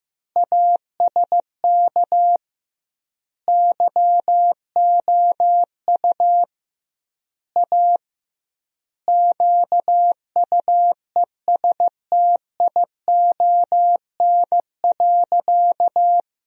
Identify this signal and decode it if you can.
Morse code, heard as a keyed tone, about 16 wpm, 700 Hz. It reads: ASK YOU A QUESTION.
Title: 2x Word-Spacing - Sentences from Top 400 Words - 15wpm